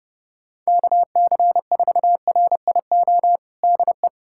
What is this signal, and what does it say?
Morse code, heard as a keyed tone, about 30 wpm, 700 Hz. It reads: KC4RIO DE
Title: USA Callsign DE Callsign 30wpm